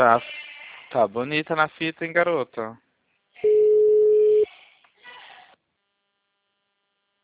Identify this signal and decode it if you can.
Ringback tone (Europe, 425 Hz), with speech.